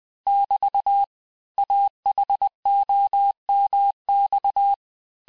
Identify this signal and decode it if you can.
Morse code, heard as a keyed tone, about 20 wpm, 800 Hz. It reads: <BT> AHOMX
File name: cours-du-lundi files / cours-1584-teste-chiffre-vitesse-19mpm